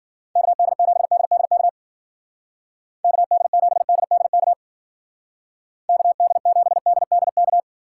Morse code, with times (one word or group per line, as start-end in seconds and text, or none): 0.4-1.7 KD7DDK
3.0-4.5 KD7DDK
5.9-7.6 KD7DDK